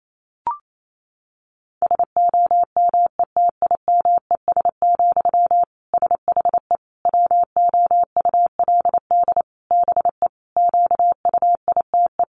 Morse code, heard as a keyed tone, about 28 wpm, 700 Hz. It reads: SOMETIMES, SHE WOULD BE QUITE
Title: Sentences from Top 1000 Words 28wpm